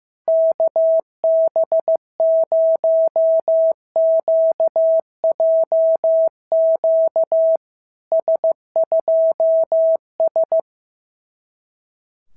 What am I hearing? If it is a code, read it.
Morse code, heard as a keyed tone, about 15 wpm, 650 Hz. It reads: KB0QJQ S2S
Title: SOTA QSO Practice 15wpm